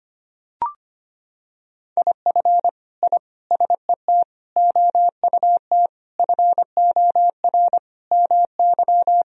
Morse code, heard as a keyed tone, about 25 wpm, 700 Hz. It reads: IF I SET OUT FOR MY
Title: Sentences from Top 1000 Words - Review Part 2 - 25wpm